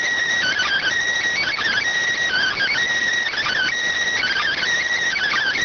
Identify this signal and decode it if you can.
Emergency Alert System (SAME) tones (header bursts), heard as harsh buzzing bursts. It is heard from 0.4 to 4.6 s.